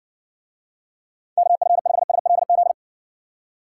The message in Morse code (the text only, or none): KF5SLZ